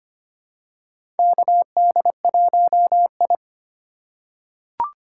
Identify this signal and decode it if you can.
Morse code, heard as a keyed tone, about 25 wpm, 700 Hz. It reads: KD1I
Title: USA Call Signs - 4x Character Spacing - 25wpm